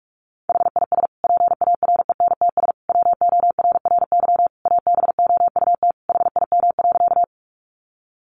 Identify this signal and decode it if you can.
Morse code, heard as a keyed tone, about 44 wpm, 700 Hz. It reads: HIS PARENTS WORRY ABOUT HIM.
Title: Easy Phrases 45wpm